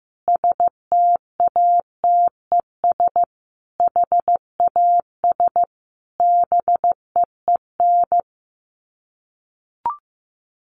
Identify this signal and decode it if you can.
Morse code, heard as a keyed tone, about 15 wpm, 700 Hz. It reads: STATES HAS BEEN
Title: Sets of 5 Words Encore2 15wpm